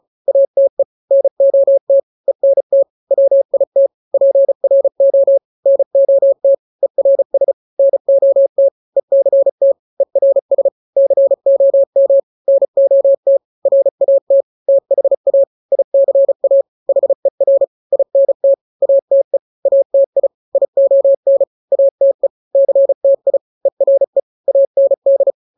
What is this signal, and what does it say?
Morse code, heard as a keyed tone, about 34 wpm, 550 Hz. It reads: ATE NOT ENT WIT PRO NOT ERS NOT ECT ERS COM NOT RAT THA ICA HER INT ATE ATI ION ATE CTI ERE AND
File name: 3-grams 35 30 WPM